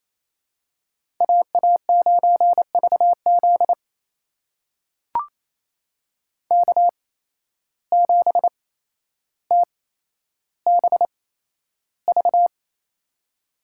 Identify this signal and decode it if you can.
Morse code, heard as a keyed tone, about 28 wpm, spaced as for 12 wpm, 700 Hz. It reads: AA9VZ K7TBV
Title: USA Call Signs - 5 Characters - 8x Character Spacing - 28wpm